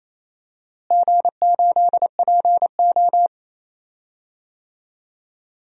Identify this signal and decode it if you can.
Morse code, heard as a keyed tone, about 28 wpm, 700 Hz. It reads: G8PO
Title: DXCC-100 Call Signs - 4 Characters - Repeated 3x - 28wpm